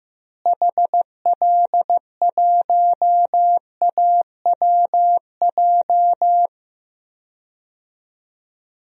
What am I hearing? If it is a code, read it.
Morse code, heard as a keyed tone, about 15 wpm, 700 Hz. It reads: HL1AWJ
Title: DXCC-100 Call Signs - 6 Characters - Repeated 2x - 15wpm